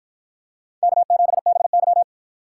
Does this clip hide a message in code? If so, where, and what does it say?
Yes: Morse code from 0.8 to 2.0 s, heard as a keyed tone, about 53 wpm, 700 Hz. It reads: K7BY